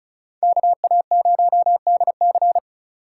Morse code, heard as a keyed tone, about 34 wpm, 700 Hz. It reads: KA0DC
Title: USA Call Signs - 3x Character Spacing - 35wpm